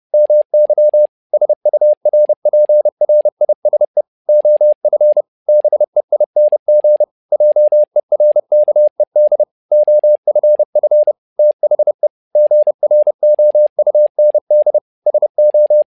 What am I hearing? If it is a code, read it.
Morse code, heard as a keyed tone, about 30 wpm, 600 Hz. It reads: MY SURPRISE OF BEING JERKED OFF THE GROUND SO
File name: learning center / tower trouble / tower trouble 30wpm